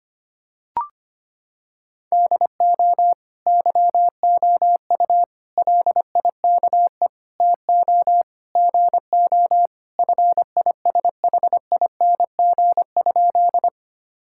Morse code, heard as a keyed tone, about 25 wpm, 700 Hz. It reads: DO YOU LIKE TO GO FISHING?
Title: Sentences from Top 1000 Words - Review Part 5 - 25wpm